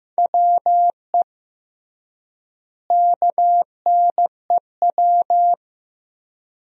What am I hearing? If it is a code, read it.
Morse code, heard as a keyed tone, about 15 wpm, 700 Hz. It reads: WE KNEW
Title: 3x Word-Spacing - Sentences from Top 400 Words - 15wpm